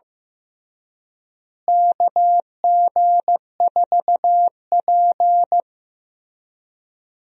Morse code, heard as a keyed tone, about 15 wpm, 700 Hz. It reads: KG4P
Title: USA Call Signs - 4 Characters - Repeated 3x - 15wpm